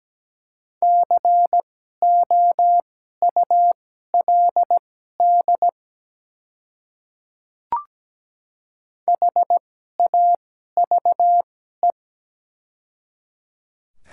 Morse code, heard as a keyed tone, about 17 wpm, spaced as for 12 wpm, 700 Hz. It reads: COULD HAVE
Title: 2x Character Spacing - Top 200 Words - 17wpm